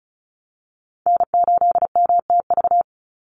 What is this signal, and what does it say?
Morse code, heard as a keyed tone, about 34 wpm, 700 Hz. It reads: N8MTV
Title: Call Signs - 35wpm